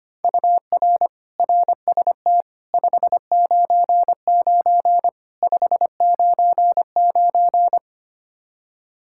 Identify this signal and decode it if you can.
Morse code, heard as a keyed tone, about 25 wpm, 700 Hz. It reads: UR RST 599 599
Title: Practice QSO - Exchange 1 - Beginning - 25wpm